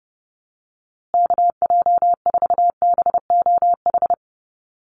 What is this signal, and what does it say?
Morse code, heard as a keyed tone, about 30 wpm, 700 Hz. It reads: KJ4BOH
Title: USA Call Signs Repeated 2x 30wpm